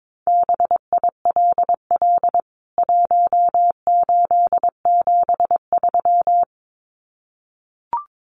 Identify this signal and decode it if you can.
Morse code, heard as a keyed tone, about 22 wpm, 700 Hz. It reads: BILL 1873
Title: CWT Exchange 22wpm